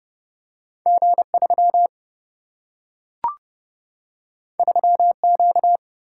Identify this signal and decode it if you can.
Morse code, heard as a keyed tone, about 30 wpm, 700 Hz. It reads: G 3 3 Q
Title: Single-Letter Single-Number 30wpm